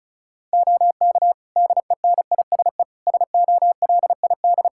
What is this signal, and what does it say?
Morse code, heard as a keyed tone, about 36 wpm, 700 Hz. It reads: OK DENISE SOLID
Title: Practice QSO - Exchange 3 - More Info - 35wpm